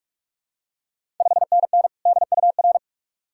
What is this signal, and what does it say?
Morse code, heard as a keyed tone, about 46 wpm, 700 Hz. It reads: 5NN DUR
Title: Ontario QSO Party Exchange 45wpm